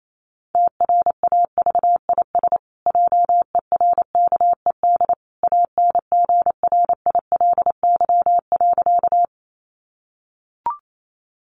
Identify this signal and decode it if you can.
Morse code, heard as a keyed tone, about 28 wpm, 700 Hz. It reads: TRAVIS JERKED ANGRILY.
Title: A Sound Of Thunder 28wpm